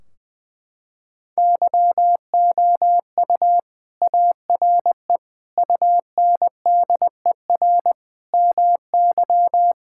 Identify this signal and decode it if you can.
Morse code, heard as a keyed tone, about 20 wpm, 700 Hz. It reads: YOU ARE UNDER MY
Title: Sentences from Top 1600 Words 20wpm